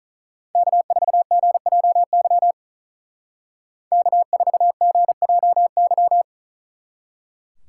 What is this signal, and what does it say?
Morse code, heard as a keyed tone, about 41 wpm, 700 Hz. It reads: K4GJY K4GJY
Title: Speed Racing - USA 1x3 Call Signs 35wpm